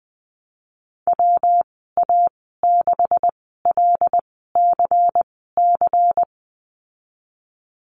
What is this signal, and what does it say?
Morse code, heard as a keyed tone, about 20 wpm, spaced as for 15 wpm, 700 Hz. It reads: WA6LCC
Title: 154 Lesson 25g - Letter p - Call Signs 20wpm